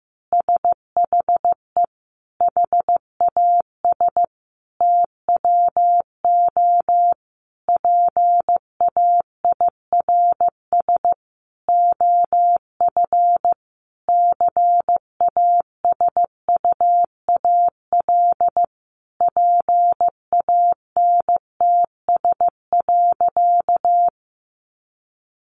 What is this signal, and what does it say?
Morse code, heard as a keyed tone, about 15 wpm, 700 Hz. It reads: SHE HAS TWO PAIRS OF CASUAL PANTS.
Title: Additional Easy Phrases 15wpm